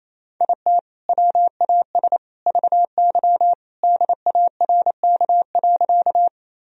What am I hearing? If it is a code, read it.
Morse code, heard as a keyed tone, about 28 wpm, 700 Hz. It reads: IT WAS VY DARK.
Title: Sentences from Top 500 Words and Abbreviations 28wpm